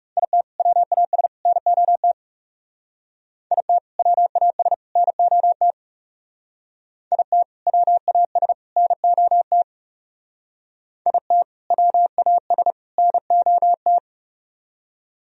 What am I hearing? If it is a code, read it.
Morse code, heard as a keyed tone, about 36 wpm, 700 Hz. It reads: IT WAS NOT IT WAS NOT IT WAS NOT IT WAS NOT
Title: Speed-Racing - Sets of 3 Words 30wpm